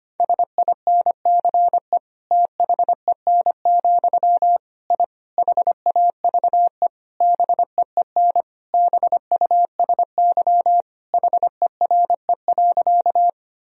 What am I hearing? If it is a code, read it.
Morse code, heard as a keyed tone, about 25 wpm, 700 Hz. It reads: SINCE THEN, I HAVE BEEN BUSY HERE.
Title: Sentences from Top 1000 Words - Review Part 3 - 25wpm